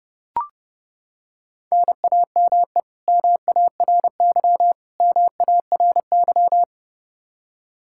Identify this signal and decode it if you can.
Morse code, heard as a keyed tone, about 29 wpm, 700 Hz. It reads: NAME MARY MARY
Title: K1USN SST QSO Practice 30wpm